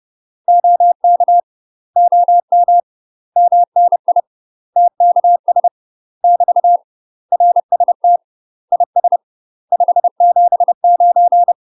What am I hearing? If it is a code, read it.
Morse code, heard as a keyed tone, about 30 wpm, 700 Hz. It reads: OK OM MNI TKS <BT> RST IS 579
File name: UK MASTER QSOs TRAINING / 30WPM / UK MASTER QSOs 0036